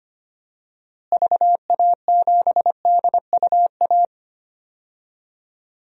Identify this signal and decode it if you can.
Morse code, heard as a keyed tone, about 25 wpm, 700 Hz. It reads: VA7DUA
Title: DXCC-100 Call Signs - 6 Characters - Repeated 2x - 25wpm